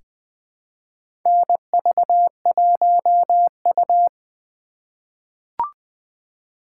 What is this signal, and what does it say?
Morse code, heard as a keyed tone, about 21 wpm, 700 Hz. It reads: NV1U
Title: K1USN SST QSO Practice - 20wpm